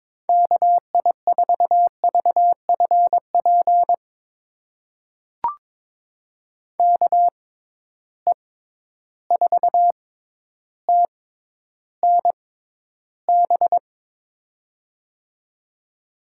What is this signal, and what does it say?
Morse code, heard as a keyed tone, about 22 wpm, spaced as for 11 wpm, 700 Hz. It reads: KI4VFP KE4TNB
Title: USA Call Signs - 6 Characters - 6x Character Spacing - 22wpm